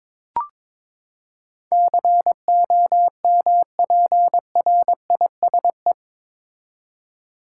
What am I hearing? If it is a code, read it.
Morse code, heard as a keyed tone, about 21 wpm, 700 Hz. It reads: COMPRISE
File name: practice / 22wpm / Eight Letter Words - 22wpm